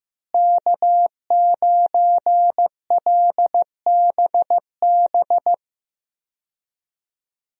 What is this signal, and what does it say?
Morse code, heard as a keyed tone, about 15 wpm, 700 Hz. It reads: K9LBB
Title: USA Call Signs - 5 Characters - Repeated 2x - 15wpm